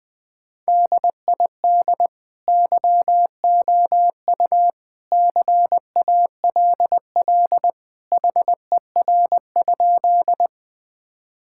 Morse code, heard as a keyed tone, about 20 wpm, 700 Hz. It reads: DID YOU CALL HER?
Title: Sentences with 2-Word Phrases - 20wpm